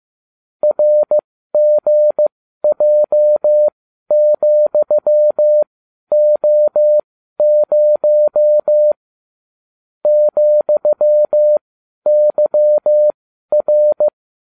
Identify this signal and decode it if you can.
Morse code, heard as a keyed tone, about 15 wpm, spaced as for 12 wpm, 600 Hz. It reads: RGJ,O0 ,YR
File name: cw / ubacw30 3